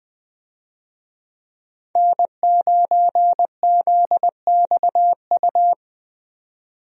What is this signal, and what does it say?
Morse code, heard as a keyed tone, about 20 wpm, 700 Hz. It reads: N9ZXU
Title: USA Call Signs - 5 Characters - 20wpm